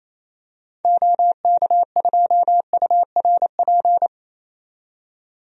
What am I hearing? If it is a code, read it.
Morse code, heard as a keyed tone, about 28 wpm, 700 Hz. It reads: OK2URP
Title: DXCC-100 Call Signs - 6 Characters - 28wpm